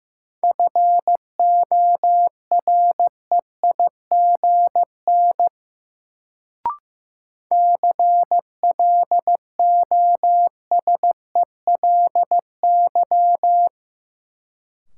Morse code, heard as a keyed tone, about 15 wpm, 700 Hz. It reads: FOREIGN CLOSELY
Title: Sentences from Top 1400 Words - New Word Drill - 15wpm